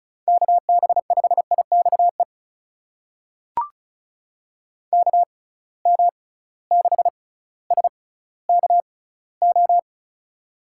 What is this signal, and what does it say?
Morse code, heard as a keyed tone, about 36 wpm, spaced as for 18 wpm, 700 Hz. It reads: KB5IXE KM6SKO
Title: USA Call Signs - 6 Characters - 6x Character Spacing - 35wpm